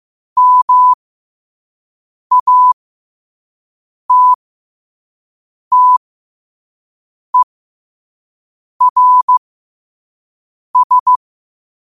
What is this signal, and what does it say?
Morse code, heard as a keyed tone, about 15 wpm, spaced as for 5 wpm, 1000 Hz. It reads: MATTERS
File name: qotdfiles / 5wpm